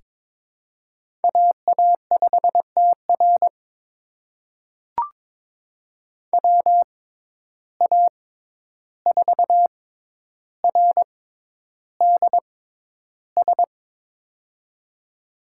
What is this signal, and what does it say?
Morse code, heard as a keyed tone, about 21 wpm, spaced as for 11 wpm, 700 Hz. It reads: AA5TR WA4RDS
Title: USA Call Signs - 6x Character Spacing - 22wpm